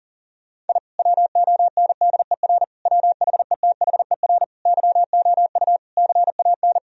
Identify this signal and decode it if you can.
Morse code, heard as a keyed tone, about 37 wpm, 700 Hz. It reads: I WONDER WHETHER YOU CAN
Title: Sentences from Top 1000 Words - Review Part 3 - 40wpm